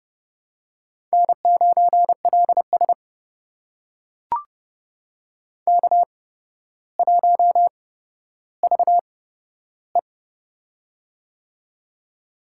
Morse code, heard as a keyed tone, about 30 wpm, spaced as for 13 wpm, 700 Hz. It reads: N9LS K1VE
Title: USA Call Signs - 4 Characters - 8x Character Spacing - 30wpm